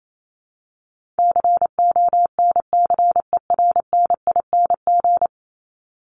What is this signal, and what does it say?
Morse code, heard as a keyed tone, about 28 wpm, 700 Hz. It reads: CONCERNING
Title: Top 1751-2000 Words 28wpm